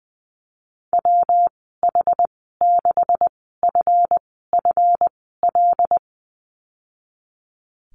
Morse code, heard as a keyed tone, about 20 wpm, spaced as for 15 wpm, 700 Hz. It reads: WH6FFL
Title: 173 Lesson 28g - Number 9 - Call Signs 20wpm